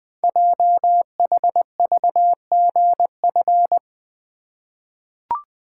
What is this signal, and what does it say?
Morse code, heard as a keyed tone, about 21 wpm, 700 Hz. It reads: JHVGF
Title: Code Groups - 5 Letters - 20wpm